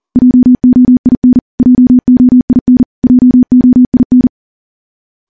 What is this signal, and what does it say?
Morse code, heard as a keyed tone, about 39 wpm, 250 Hz. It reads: JOIN JOIN JOIN
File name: morse / morse-training-40WPM